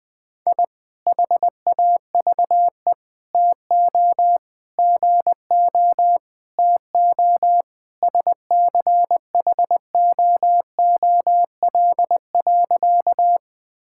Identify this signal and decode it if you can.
Morse code, heard as a keyed tone, about 21 wpm, 700 Hz. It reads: I HAVE TO GO TO SCHOOL.
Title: Sentences with 3-Word Phrases - 20wpm